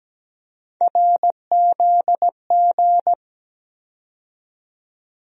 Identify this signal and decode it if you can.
Morse code, heard as a keyed tone, about 17 wpm, 700 Hz. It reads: RZG
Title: USA Call Sign Postfixes - 3 Letters 17wpm